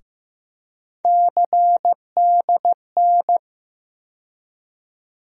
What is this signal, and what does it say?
Morse code, heard as a keyed tone, about 15 wpm, 700 Hz. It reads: CDN
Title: USA Call Sign Postfixes - 3 Letters 15wpm